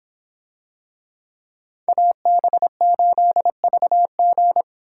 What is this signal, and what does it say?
Morse code, heard as a keyed tone, about 26 wpm, 700 Hz. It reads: AB8VG